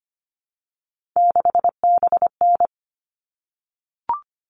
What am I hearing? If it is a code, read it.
Morse code, heard as a keyed tone, about 25 wpm, 700 Hz. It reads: 6BN
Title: All Cut Numbers - 3-Digit 25wpm